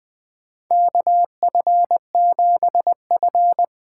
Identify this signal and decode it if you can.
Morse code, heard as a keyed tone, about 20 wpm, 700 Hz. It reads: KF7F
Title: USA Call Signs - 6 Characters - 8x Character Spacing - 20wpm